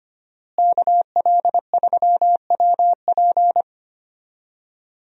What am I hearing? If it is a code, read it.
Morse code, heard as a keyed tone, about 25 wpm, 700 Hz. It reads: KL3WP
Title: USA Call Signs - 5 Characters - Repeated 2x - 25wpm